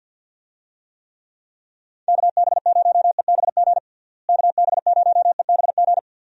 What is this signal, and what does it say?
Morse code, heard as a keyed tone, about 47 wpm, 700 Hz. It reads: KB0EBG KB0EBG
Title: USA Call Signs - 6 Characters - Repeated 2x - 50wpm